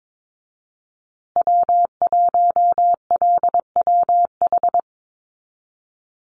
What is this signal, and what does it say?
Morse code, heard as a keyed tone, about 22 wpm, 700 Hz. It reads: W1LWH
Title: Call Signs Repeated 3x - 22wpm